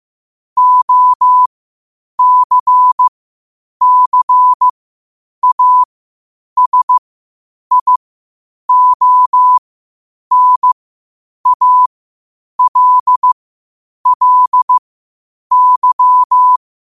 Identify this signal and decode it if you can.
Morse code, heard as a keyed tone, about 15 wpm, spaced as for 8 wpm, 1000 Hz. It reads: OCCASIONALLY
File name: qotdfiles / 8wpm